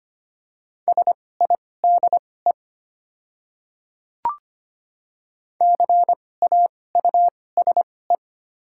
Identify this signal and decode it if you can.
Morse code, heard as a keyed tone, about 25 wpm, spaced as for 18 wpm, 700 Hz. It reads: SIDE CAUSE
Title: 2x Character Spacing - Top 200 Words - 25wpm